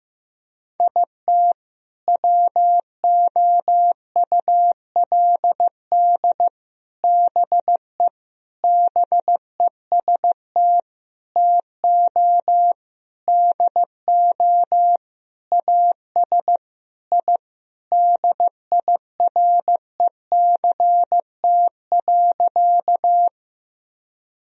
Morse code, heard as a keyed tone, about 15 wpm, 700 Hz. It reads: IT WOULD BE BEST TO DO AS I DIRECT.
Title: Sentences from Top 1000 Words - Review Part 2 - 15wpm